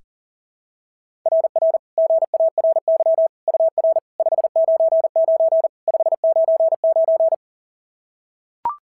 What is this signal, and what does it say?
Morse code, heard as a keyed tone, about 39 wpm, 650 Hz. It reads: RR GARY UR 599 599
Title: SOTA QSO Practice 40wpm